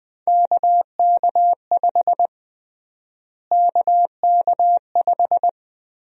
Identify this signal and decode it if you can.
Morse code, heard as a keyed tone, about 20 wpm, 700 Hz. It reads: KK5 KK5
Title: USA Call Sign Prefixes - Repeated 2x - 20wpm